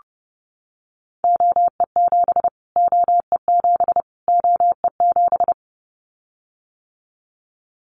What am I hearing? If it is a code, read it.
Morse code, heard as a keyed tone, about 29 wpm, 700 Hz. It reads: OE7 OE7 OE7
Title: Callsign Prefixes for Top 25 DXCC - Repeated 3x - 30wpm